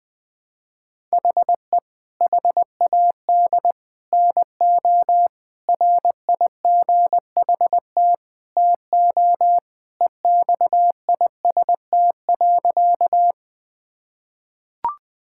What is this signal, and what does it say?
Morse code, heard as a keyed tone, about 20 wpm, 700 Hz. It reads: HE HAD NO RIGHT TO EXIST.
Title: Sentences from Top 1000 Words - Review Part 5 - 20wpm